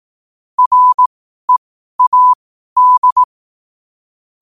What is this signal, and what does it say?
Morse code, heard as a keyed tone, about 18 wpm, spaced as for 12 wpm, 1000 Hz. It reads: READ
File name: qotdfiles / 12wpm